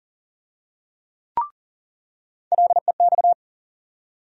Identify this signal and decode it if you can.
Morse code, heard as a keyed tone, about 38 wpm, 700 Hz. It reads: LEX
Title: USA Call Sign Postfixes - 3 Letters 40wpm